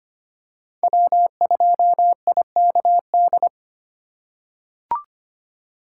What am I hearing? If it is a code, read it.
Morse code, heard as a keyed tone, about 25 wpm, 700 Hz. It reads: W2IKD
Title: USA Call Signs - 4x Character Spacing - 25wpm